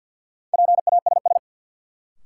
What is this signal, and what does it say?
Morse code, heard as a keyed tone, about 53 wpm, 700 Hz. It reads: PASS
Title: Sentences from Top 400 Words - New Word Drill - 50wpm